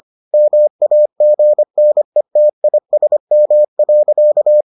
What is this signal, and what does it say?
Morse code, heard as a keyed tone, about 25 wpm, 600 Hz. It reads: MAGNETISM.